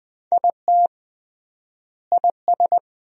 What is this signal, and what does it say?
Morse code, heard as a keyed tone, about 20 wpm, 700 Hz. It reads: IT IS
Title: 3x Word-Spacing - Sentences from Top 200 Words - 20wpm